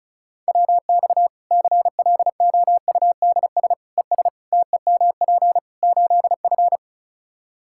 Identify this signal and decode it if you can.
Morse code, heard as a keyed tone, about 36 wpm, 700 Hz. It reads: WX CLOUDS ES TEMP 8F
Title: Practice QSO - Exchange 2 - Info - 35wpm